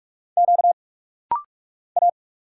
Morse code, heard as a keyed tone, about 46 wpm, 700 Hz. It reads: Q A
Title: Single Letters - Rapid-Fire 45wpm